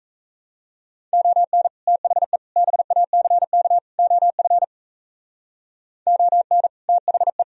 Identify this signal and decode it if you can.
Morse code, heard as a keyed tone, about 35 wpm, 700 Hz. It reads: ON THE BACK OF ON THE